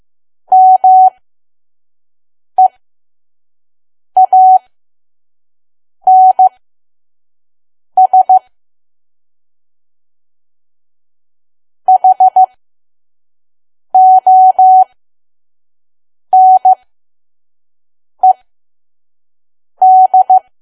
Morse code, heard as a keyed tone, about 15 wpm, spaced as for 5 wpm, 750 Hz. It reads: MEANS HONED
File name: Archive / 5 WPM files / PM